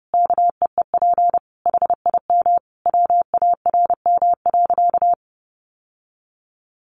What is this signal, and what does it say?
Morse code, heard as a keyed tone, about 30 wpm, 700 Hz. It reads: KEEP HIM WARM.
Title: Sentences from Top 1500 Words 30wpm